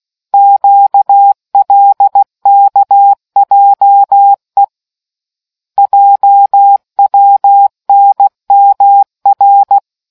Morse code, heard as a keyed tone, about 16 wpm, 800 Hz. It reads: QLKJE JWNMR